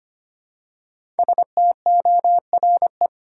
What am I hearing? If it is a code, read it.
Morse code, heard as a keyed tone, about 25 wpm, 700 Hz. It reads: STORE
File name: 5 Letter Words - First 3 Letters Given - 25wpm